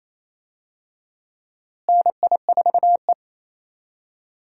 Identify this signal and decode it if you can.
Morse code, heard as a keyed tone, about 27 wpm, 700 Hz. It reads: NI4E
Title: USA Call Signs - 4 Characters - Repeated 2x - 28wpm